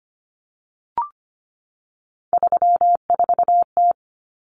Morse code, heard as a keyed tone, about 25 wpm, 700 Hz. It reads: 34T
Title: Common Cut Numbers - 3-Digit 25wpm